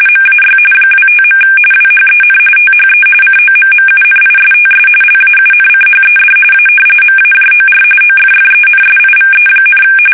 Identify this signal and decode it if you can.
RTTY, about 45.45 baud. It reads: MAY 25. WHEN THE COMMUNISTS LAUNCHED A MAJOR OFFENSIVE,
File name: fsk850 / battleshipnj